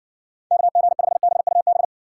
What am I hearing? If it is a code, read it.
Morse code, heard as a keyed tone, about 60 wpm, 700 Hz. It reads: KG5BUB